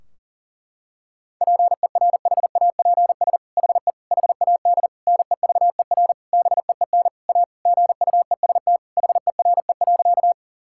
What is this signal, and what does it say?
Morse code, heard as a keyed tone, about 37 wpm, 700 Hz. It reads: PERHAPS HE HAD NEVER BEEN A GUEST HERE.
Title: Sentences from Top 1600 Words 40wpm